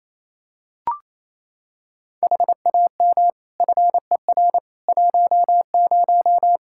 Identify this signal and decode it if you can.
Morse code, heard as a keyed tone, about 28 wpm, 700 Hz. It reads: HAM FER 10
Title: Practice QSO - Exchange 3 - More Info - 28wpm